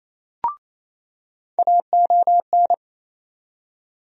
Morse code, heard as a keyed tone, about 28 wpm, 700 Hz. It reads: AON
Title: USA Call Sign Postfixes - 3 Letters 28wpm